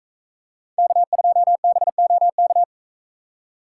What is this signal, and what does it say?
Morse code, heard as a keyed tone, about 41 wpm, 700 Hz. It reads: K2BOK